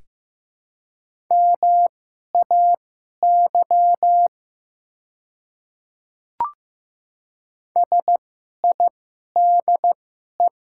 Morse code, heard as a keyed tone, about 15 wpm, spaced as for 11 wpm, 700 Hz. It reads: MAY SIDE
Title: 2x Character Spacing - Top 300 Words - 15wpm